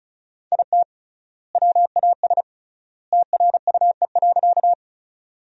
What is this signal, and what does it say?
Morse code, heard as a keyed tone, about 36 wpm, 700 Hz. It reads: IT WAS TRUE.
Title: 3x Word-Spacing - Sentences from Top 400 Words - 35wpm